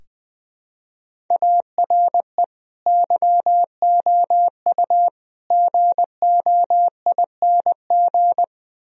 Morse code, heard as a keyed tone, about 21 wpm, 700 Hz. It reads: ARE YOU GOING
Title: Sentences from Top 1000 Words - Review Part 1 - 20wpm